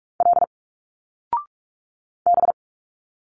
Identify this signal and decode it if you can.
Morse code, heard as a keyed tone, about 46 wpm, 700 Hz. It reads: L B
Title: Single Letters 45wpm